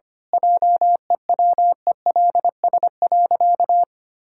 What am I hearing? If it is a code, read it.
Morse code, heard as a keyed tone, about 25 wpm, 700 Hz. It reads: JEWELS.
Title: Pangram Challenge v2 - 25wpm